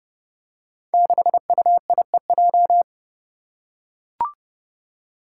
Morse code, heard as a keyed tone, about 29 wpm, 700 Hz. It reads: 6UIEJ
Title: Five Challenging Letter-Number Combinations 30wpm